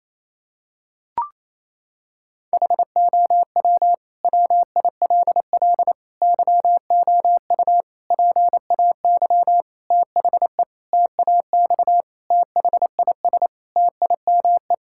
Morse code, heard as a keyed tone, about 28 wpm, 700 Hz. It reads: HOW WILL YOU PAY THE TAX THIS TIME
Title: Sentences from Top 1000 Words - Review Part 5 - 28wpm